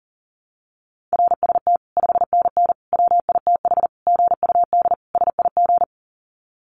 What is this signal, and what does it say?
Morse code, heard as a keyed tone, about 40 wpm, 700 Hz. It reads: RST 5NN WITH GUD SIG
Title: Common QSO Phrases 40wpm